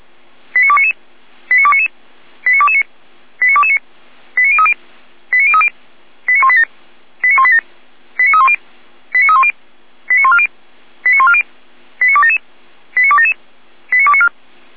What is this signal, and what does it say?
Five-tone selective call (ZVEI), heard as quick runs of five tones, tones of about 80 ms. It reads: 89299 89299 89229 89229 89030 89030 89187 89187 89310 89310 89140 89140 89277 89277 89294